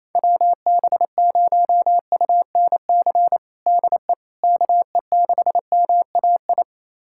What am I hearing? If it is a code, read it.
Morse code, heard as a keyed tone, about 28 wpm, 700 Hz. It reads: WB0UNC DE KE6MAI
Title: USA Callsign DE Callsign 28wpm